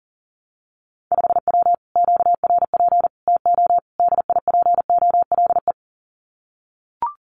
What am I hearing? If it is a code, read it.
Morse code, heard as a keyed tone, about 40 wpm, 700 Hz. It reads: 5W QRP TO DIPOLE
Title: Common QSO Phrases Encore 40wpm